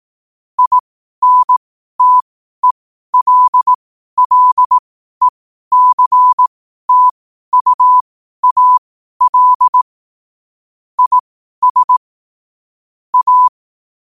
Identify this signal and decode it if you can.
Morse code, heard as a keyed tone, about 18 wpm, spaced as for 12 wpm, 1000 Hz. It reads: INTELLECTUAL IS A
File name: qotdfiles / 12wpm